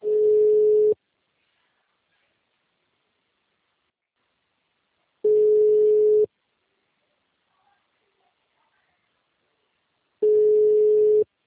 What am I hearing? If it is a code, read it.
Ringback tone (Europe, 425 Hz).
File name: out- -2023- - - .737